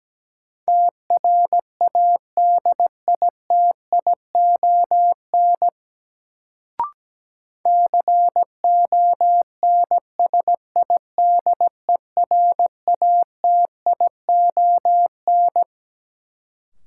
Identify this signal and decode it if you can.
Morse code, heard as a keyed tone, about 17 wpm, 700 Hz. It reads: TRADITION CONSIDERATION
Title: Sentences from Top 1500 Words - New Word Drill - 17wpm